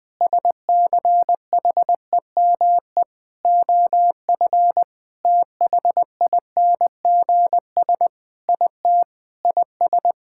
Morse code, heard as a keyed tone, about 20 wpm, 700 Hz. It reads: SCHEME OF THINGS IT IS
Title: Sentences with 6-Word Phrases - Spoken Beginning - 20wpm